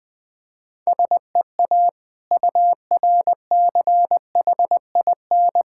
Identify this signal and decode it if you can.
Morse code, heard as a keyed tone, about 21 wpm, 690 Hz. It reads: SEA URCHIN
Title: Difficult Words with Hints - 20wpm